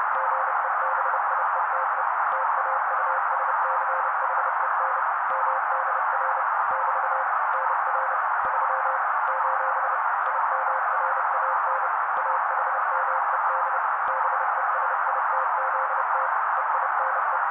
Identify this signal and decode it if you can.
Morse code, heard as a keyed tone, about 30 wpm, 530 Hz. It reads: GEBIETE TAUSCHEN MDR <BT> NR 3 : IMRAN AHMED BESITZT EINE
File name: daily / 29wpm noise